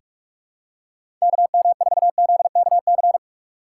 Morse code, heard as a keyed tone, about 43 wpm, 700 Hz. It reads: KM4ZKC
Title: USA Call Signs - 6 Characters - 2x Character Spacing - 45wpm